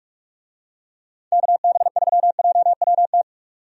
Morse code, heard as a keyed tone, about 46 wpm, 700 Hz. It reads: KB3JWT
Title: USA Call Signs - 6 Characters - 45wpm